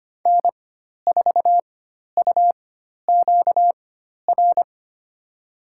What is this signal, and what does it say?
Morse code, heard as a keyed tone, about 25 wpm, spaced as for 12 wpm, 700 Hz. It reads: N4UQR
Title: USA Call Signs - 5 Characters - 4x Character Spacing - 25wpm